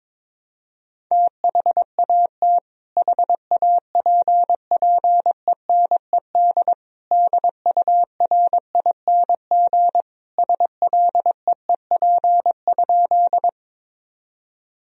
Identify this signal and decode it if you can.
Morse code, heard as a keyed tone, about 22 wpm, 700 Hz. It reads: THAT HAPPENED DURING SLEEP?
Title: Sentences from Top 1000 Words - Review Part 2 - 22wpm